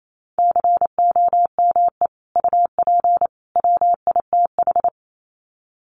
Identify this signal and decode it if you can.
Morse code, heard as a keyed tone, about 28 wpm, 700 Hz. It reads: COME UP WITH
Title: Sets of 3 Words 28wpm